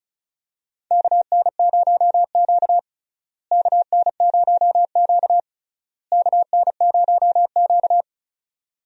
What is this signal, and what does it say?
Morse code, heard as a keyed tone, about 36 wpm, 700 Hz. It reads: KN0Q KN0Q KN0Q
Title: USA Call Signs - 4 Characters - Repeated 3x - 35wpm